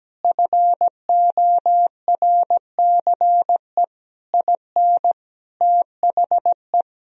Morse code, heard as a keyed tone, about 17 wpm, 700 Hz. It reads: FORCE IN THE
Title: Sentences from Top 1000 Words - Review Part 4 - 17wpm